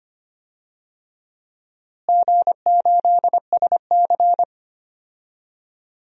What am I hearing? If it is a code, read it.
Morse code, heard as a keyed tone, about 25 wpm, 700 Hz. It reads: G8SC
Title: DXCC-100 Call Signs - 4 Characters - Repeated 3x - 25wpm